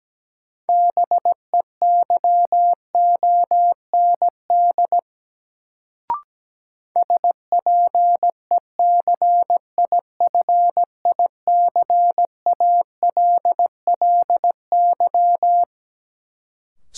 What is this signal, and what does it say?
Morse code, heard as a keyed tone, about 17 wpm, 700 Hz. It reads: BEYOND SPECIFICALLY
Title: Sentences from Top 1100 Words - New Word Drill Alt - 17wpm